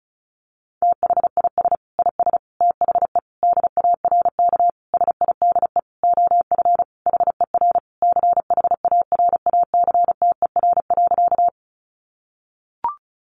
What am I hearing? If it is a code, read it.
Morse code, heard as a keyed tone, about 34 wpm, 700 Hz. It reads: THIS IS THE DARK SIDE OF HER CHARACTER.
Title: Sentences from Top 1200 Words 35wpm